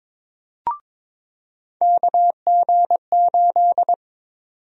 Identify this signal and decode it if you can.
Morse code, heard as a keyed tone, about 22 wpm, 700 Hz. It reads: KG8
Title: USA Call Sign Prefixes - 22wpm